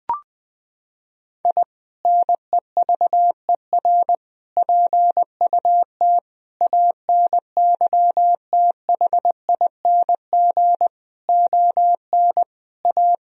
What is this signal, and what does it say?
Morse code, heard as a keyed tone, about 20 wpm, 700 Hz. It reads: I NEVER PUT ANYTHING ON A
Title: Sentences from Top 1000 Words - Review Part 1 - 20wpm